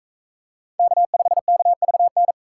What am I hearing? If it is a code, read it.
Morse code, heard as a keyed tone, about 40 wpm, 700 Hz. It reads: K5KVN